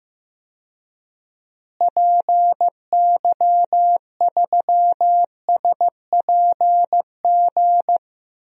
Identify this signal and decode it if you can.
Morse code, heard as a keyed tone, about 15 wpm, 700 Hz. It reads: PY3SPG
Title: DXCC-100 Call Signs - 6 Characters - Repeated 2x - 15wpm